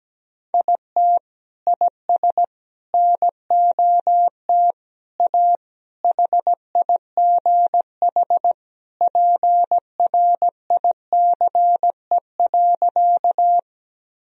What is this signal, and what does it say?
Morse code, heard as a keyed tone, about 17 wpm, 700 Hz. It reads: IT IS NOT A HIGH PRICE.
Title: Sentences from Top 1000 Words - Review Part 4 - 17wpm